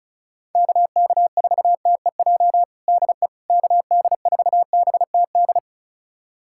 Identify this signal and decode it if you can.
Morse code, heard as a keyed tone, about 36 wpm, 700 Hz. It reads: KK4TEJ DE KD4BTD
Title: USA Callsign DE Callsign - 6 Characters - 35wpm